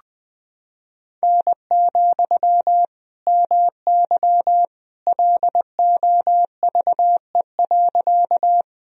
Morse code, heard as a keyed tone, about 19 wpm, 700 Hz. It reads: N, MY LOVE.
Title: Sentences from Top 400 Words and Abbreviations 20wpm